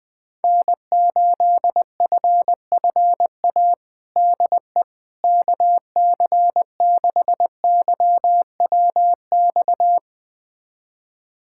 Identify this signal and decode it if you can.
Morse code, heard as a keyed tone, about 21 wpm, 700 Hz. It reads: N8FFA DE KC6YWX
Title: USA Callsign DE Callsign 20wpm